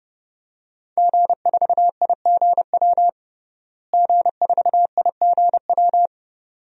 Morse code, heard as a keyed tone, about 30 wpm, 700 Hz. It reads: G4IGW G4IGW
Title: DXCC-100 Call Signs - 5 Characters - Repeated 2x - 30wpm